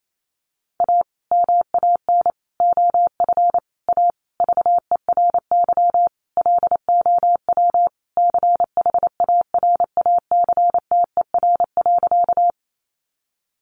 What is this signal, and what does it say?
Morse code, heard as a keyed tone, about 28 wpm, 700 Hz. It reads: A MAN OF A VERY LOW CHARACTER.
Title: Sentences from Top 1200 Words 28wpm